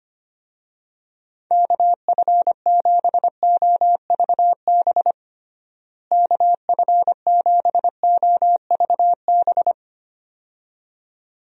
Morse code, heard as a keyed tone, about 25 wpm, 700 Hz. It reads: KF7OVB KF7OVB
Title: USA Call Signs - 6 Characters - Repeated 2x - 25wpm